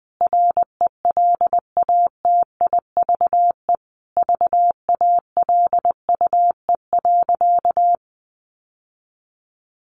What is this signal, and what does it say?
Morse code, heard as a keyed tone, about 19 wpm, 700 Hz. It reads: RELATIVE VALUE.
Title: Sentences from Top 1400 Words 20wpm